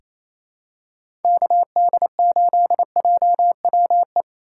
Morse code, heard as a keyed tone, about 28 wpm, 700 Hz. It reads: KD8JWE
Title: USA Call Signs - 4x Character Spacing - 28wpm